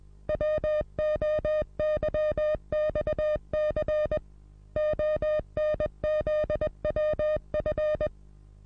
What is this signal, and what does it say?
Morse code, heard as a keyed tone, about 21 wpm, 610 Hz. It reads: WOYXC ONZWF
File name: repos / Morsekurs / DJ4UF Morsen26